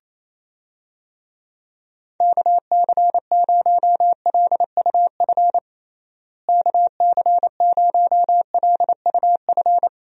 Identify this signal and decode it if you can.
Morse code, heard as a keyed tone, about 27 wpm, 700 Hz. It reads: KC0LUF KC0LUF
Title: USA Call Signs - 6 Characters - Repeated 2x - 28wpm